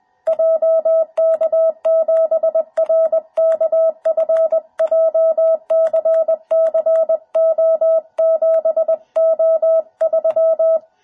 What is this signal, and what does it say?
Morse code, heard as a keyed tone, about 20 wpm, 650 Hz. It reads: JK7RKF JCCO7O3